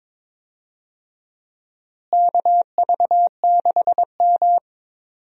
Morse code, heard as a keyed tone, about 22 wpm, 700 Hz. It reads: KV6M
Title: USA Call Signs - 4 Characters - Repeated 3x - 22wpm